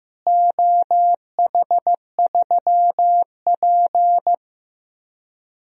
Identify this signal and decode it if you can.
Morse code, heard as a keyed tone, about 15 wpm, 700 Hz. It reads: OH3P
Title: DXCC-100 Call Signs - 4 Characters - Repeated 2x - 15wpm